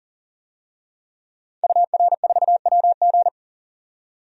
Morse code, heard as a keyed tone, about 39 wpm, 700 Hz. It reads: UR4WG
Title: DXCC-100 Call Signs - 5 Characters - 40wpm